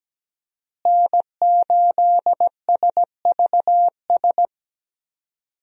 Morse code, heard as a keyed tone, about 17 wpm, 700 Hz. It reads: N8SVS
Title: USA Call Signs - 5 Characters - Repeated 2x - 17wpm